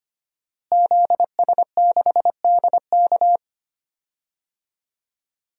Morse code, heard as a keyed tone, about 25 wpm, 700 Hz. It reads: ZS6DK
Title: DXCC-100 Call Signs - 5 Characters - Repeated 2x - 25wpm